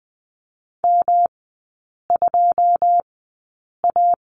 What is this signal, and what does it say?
Morse code, heard as a keyed tone, about 19 wpm, spaced as for 8 wpm, 700 Hz. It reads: M2A
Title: 133 Lesson 22e - Question Mark - Three Characters 20wpm